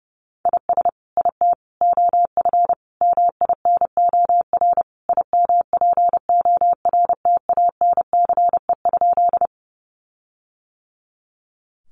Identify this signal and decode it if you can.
Morse code, heard as a keyed tone, about 30 wpm, 700 Hz. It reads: IS IT OF MINOR IMPORTANCE?
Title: Sentences from Top 1500 Words 30wpm